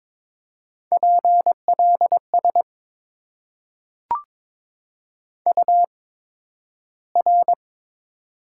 Morse code, heard as a keyed tone, about 22 wpm, spaced as for 12 wpm, 700 Hz. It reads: PLS UR
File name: Top 90 QSO Elements - 8x Character Spacing - 22wpm